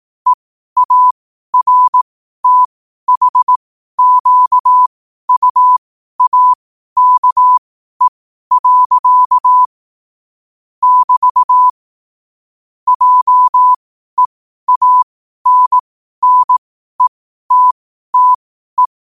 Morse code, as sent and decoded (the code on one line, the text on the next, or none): . .- .-. - .... --.- ..- .- -.- . .-.-.- / -...- / .--- . .- -. -. - - .
EARTHQUAKE. <BT> JEANNETTE